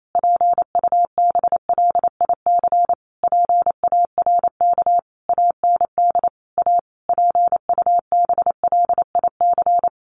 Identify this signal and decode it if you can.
Morse code, heard as a keyed tone, about 28 wpm, 700 Hz. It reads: PUBLIC PARK AND A PUBLIC
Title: Sentences from Top 1300 Words 28wpm